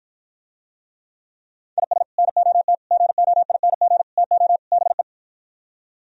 Morse code, heard as a keyed tone, about 55 wpm, 700 Hz. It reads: IS NOT GOING TO BE